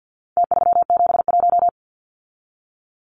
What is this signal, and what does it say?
Morse code, heard as a keyed tone, about 50 wpm, 700 Hz. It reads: T371
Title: Numbers - Contest Style - 4-Digit 50wpm